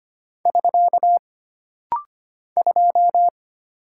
Morse code, heard as a keyed tone, about 25 wpm, 700 Hz. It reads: <SK> 2
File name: Single Character (Letter:Number:Punctuation:Prosign) - Rapid-Fire - 25wpm